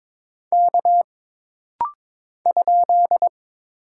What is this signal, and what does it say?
Morse code, heard as a keyed tone, about 22 wpm, 700 Hz. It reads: K ?
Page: This rapid-fire timing is geared to push your ICR (Instant Character Recognition) skills.